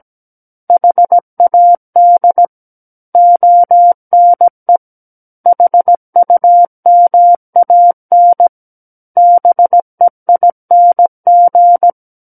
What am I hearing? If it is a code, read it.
Morse code, heard as a keyed tone, about 17 wpm, 700 Hz. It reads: HAD ONE HUMAN BEING
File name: Book One / 08-Chapter 8 - 17 WPM